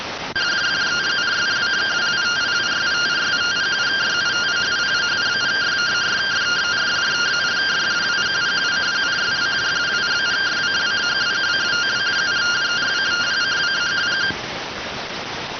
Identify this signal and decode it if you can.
Two-tone teleprinter signal, about 170 Hz shift.